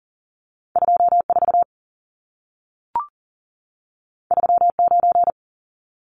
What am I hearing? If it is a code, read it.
Morse code, heard as a keyed tone, about 40 wpm, 700 Hz. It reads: 24 39
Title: Numbers - 2-Digit 40wpm